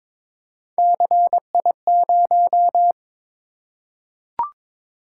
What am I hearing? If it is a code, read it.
Morse code, heard as a keyed tone, about 22 wpm, 700 Hz. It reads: CI0
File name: practice / 22wpm / Three Letter-Number Combinations - 22wpm